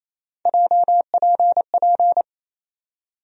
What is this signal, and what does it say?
Morse code, heard as a keyed tone, about 28 wpm, 700 Hz. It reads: JPP
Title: USA Call Sign Postfixes - 3 Letters 28wpm